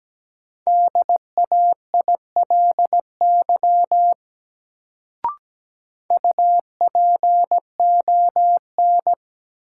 Morse code, heard as a keyed tone, about 17 wpm, 700 Hz. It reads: DAILY UPON
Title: Sentences from Top 1100 Words - New Word Drill Alt - 17wpm